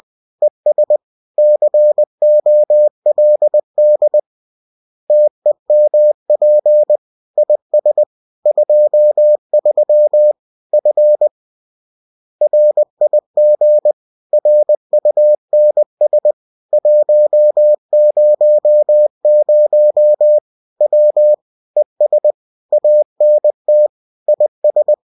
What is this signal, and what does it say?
Morse code, heard as a keyed tone, about 20 wpm, 600 Hz. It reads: ES COLD TEMP IS 23 F RIG RUNS 100 W ES ANT IS